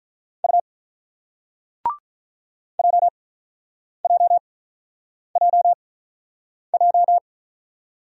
Morse code, heard as a keyed tone, about 45 wpm, spaced as for 9 wpm, 700 Hz. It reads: U JJJJ